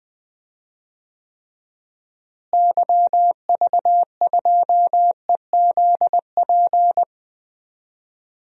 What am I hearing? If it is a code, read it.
Morse code, heard as a keyed tone, about 20 wpm, 700 Hz. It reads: YV2EZP
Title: DXCC-100 Call Signs - 6 Characters - Repeated 3x - 20wpm